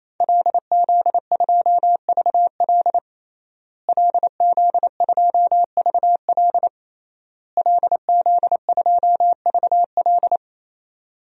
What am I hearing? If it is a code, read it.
Morse code, heard as a keyed tone, about 28 wpm, 700 Hz. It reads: LZ2VL LZ2VL LZ2VL
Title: DXCC-100 Call Signs - 5 Characters - Repeated 3x - 28wpm